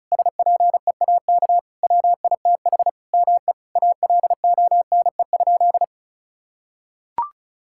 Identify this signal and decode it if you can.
Morse code, heard as a keyed tone, about 36 wpm, 690 Hz. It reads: SPEAK WITH ME ALONE?
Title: Sentences from Top 1600 Words 35wpm